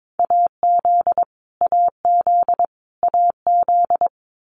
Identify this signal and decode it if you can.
Morse code, heard as a keyed tone, about 22 wpm, 700 Hz. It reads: AZ AZ AZ
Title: US States Repeated 3x 22wpm